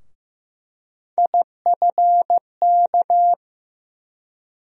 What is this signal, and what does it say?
Morse code, heard as a keyed tone, about 15 wpm, 700 Hz. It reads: IFK
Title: USA Call Sign Postfixes - 3 Letters 15wpm